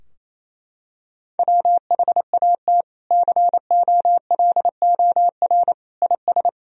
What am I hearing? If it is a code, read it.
Morse code, heard as a keyed tone, about 28 wpm, 700 Hz. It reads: WHAT COLOR IS
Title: Sentences from Top 1000 Words - Review Part 2 - 28wpm